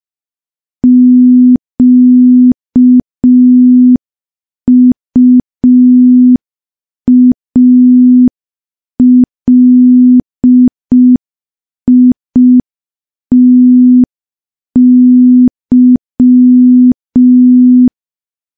Morse code, heard as a keyed tone, about 5 wpm, 250 Hz. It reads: QUALITY
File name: morse / morse-training-05WPM